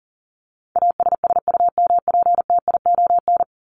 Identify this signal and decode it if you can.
Morse code, heard as a keyed tone, about 40 wpm, 700 Hz. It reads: ASSUMPTION
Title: Top 1751-2000 Words 40wpm